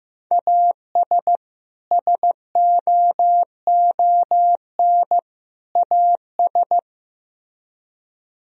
Morse code, heard as a keyed tone, about 16 wpm, 700 Hz. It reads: AS SOON AS
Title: Speed-Racing - Sets of 3 Words 15wpm